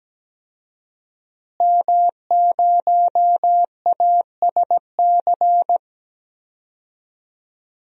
Morse code, heard as a keyed tone, about 17 wpm, 700 Hz. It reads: M0ASC
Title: DXCC-100 Call Signs - 5 Characters - Repeated 2x - 17wpm